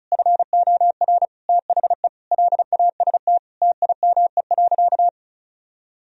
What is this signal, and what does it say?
Morse code, heard as a keyed tone, about 34 wpm, 700 Hz. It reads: FOR THE LAST TIME.
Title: Sentences from Top 1000 Words - Review Part 2 - 35wpm